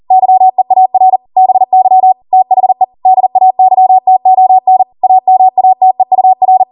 Morse code, heard as a keyed tone, about 40 wpm, 750 Hz. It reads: YEAR BY THE DAYTON AMATEUR